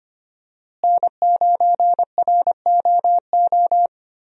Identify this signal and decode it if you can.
Morse code, heard as a keyed tone, about 25 wpm, 700 Hz. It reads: N9ROO
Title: USA Call Signs - 5 Characters - Repeated 3x - 25wpm